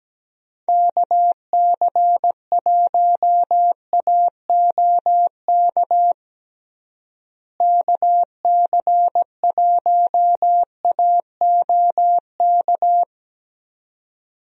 Morse code, heard as a keyed tone, about 17 wpm, 700 Hz. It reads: KC1AOK KC1AOK
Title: USA Call Signs - 6 Characters - Repeated 2x - 17wpm